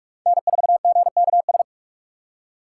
Morse code, heard as a keyed tone, about 44 wpm, 700 Hz. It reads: N4GKS
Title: USA Call Signs - 6x Character Spacing - 45wpm